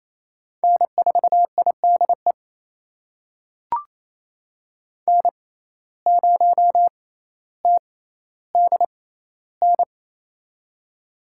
Morse code, heard as a keyed tone, about 28 wpm, spaced as for 14 wpm, 700 Hz. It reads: N4IDE N0TDN
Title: USA Call Signs - 5 Characters - 6x Character Spacing - 28wpm